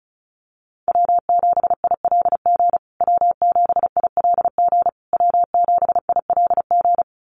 Morse code, heard as a keyed tone, about 34 wpm, 700 Hz. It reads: W7ILG W7ILG W7ILG
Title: USA Call Signs Repeated 3x 35wpm